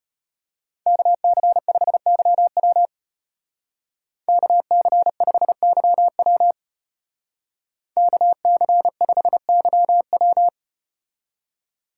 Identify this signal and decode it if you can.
Morse code, heard as a keyed tone, about 34 wpm, 700 Hz. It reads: KC5YW KC5YW KC5YW